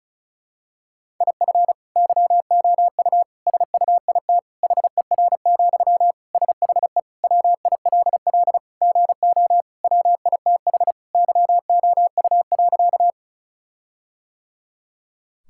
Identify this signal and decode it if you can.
Morse code, heard as a keyed tone, about 36 wpm, 700 Hz. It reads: IF YOU SUIT HER, SHE WILL GO WITH YOU.
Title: Sentences from Top 1600 Words 35wpm